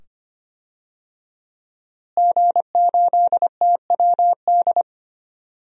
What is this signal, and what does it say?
Morse code, heard as a keyed tone, about 25 wpm, 700 Hz. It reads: G8TWD
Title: DXCC-100 Call Signs - 5 Characters - Repeated 2x - 25wpm